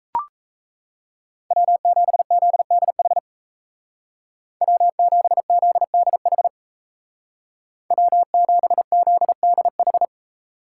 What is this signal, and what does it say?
Morse code, heard as a keyed tone, about 35 wpm, 700 Hz. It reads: W7ZDH W7ZDH W7ZDH
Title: Speed Racing - USA Call Signs 28wpm